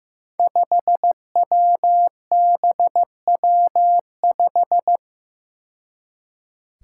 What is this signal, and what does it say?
Morse code, heard as a keyed tone, about 15 wpm, 700 Hz. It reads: 5WBW5
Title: Five Challenging Letter-Number Combinations 15wpm